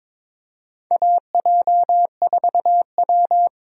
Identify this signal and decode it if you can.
Morse code, heard as a keyed tone, about 21 wpm, 700 Hz. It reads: AJ4W
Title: USA Call Signs - 4 Characters - Repeated 2x - 22wpm